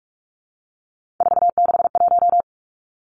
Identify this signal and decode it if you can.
Morse code, heard as a keyed tone, about 46 wpm, 700 Hz. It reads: 461
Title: Three Challenging Letter-Number Combinations 45wpm